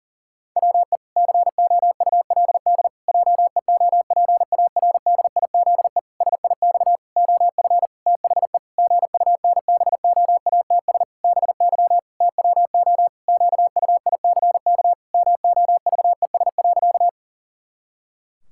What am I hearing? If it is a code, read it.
Morse code, heard as a keyed tone, about 40 wpm, 700 Hz. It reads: WE COULD JEOPARDIZE SIX OF THE GUNBOATS BY TWO QUICK MOVES.
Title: Pangram Challenge v2 - 40wpm